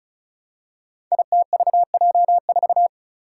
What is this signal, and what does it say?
Morse code, heard as a keyed tone, about 36 wpm, 700 Hz. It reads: ITVJ4
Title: Five Challenging Letter-Number Combinations 35wpm